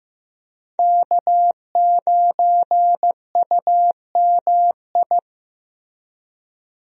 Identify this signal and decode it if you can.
Morse code, heard as a keyed tone, about 15 wpm, 700 Hz. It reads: K9UMI
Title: USA Call Signs - 5 Characters - Repeated 3x - 15wpm